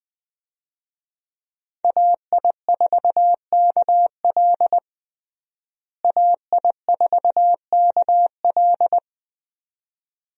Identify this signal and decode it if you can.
Morse code, heard as a keyed tone, about 20 wpm, 700 Hz. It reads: AI4KL AI4KL
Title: USA Call Signs - 5 Characters - Repeated 2x - 20wpm